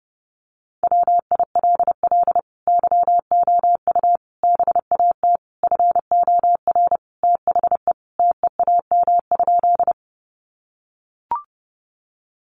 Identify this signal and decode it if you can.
Morse code, heard as a keyed tone, about 30 wpm, 700 Hz. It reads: WILL YOU BAT FOR THE TEAM?
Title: Sentences from Top 1500 Words 30wpm